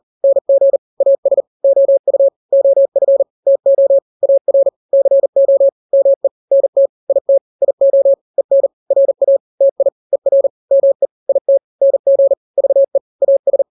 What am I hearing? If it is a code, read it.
Morse code, heard as a keyed tone, about 40 wpm, 550 Hz. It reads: NG AS OU OF TO AR CO ME NT IT IO EN RA TI ER ME IT NG VE AS
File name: n-grams / 2-grams 40 35 WPM